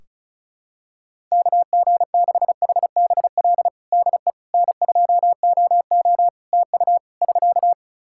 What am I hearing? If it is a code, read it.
Morse code, heard as a keyed tone, about 34 wpm, 700 Hz. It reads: KG6HBL DE N2OO TU <SK>
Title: K1USN SST QSO Practice 35wpm